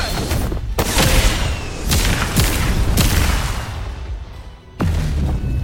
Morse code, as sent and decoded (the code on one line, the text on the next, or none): none